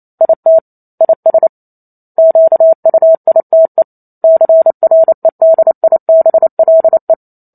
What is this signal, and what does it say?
Morse code, heard as a keyed tone, about 28 wpm, 650 Hz. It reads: IT IS QUITE CREDIBLE